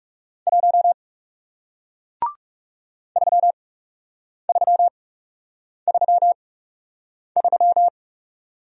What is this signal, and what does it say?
Morse code, heard as a keyed tone, about 39 wpm, spaced as for 8 wpm, 700 Hz. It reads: 1 3333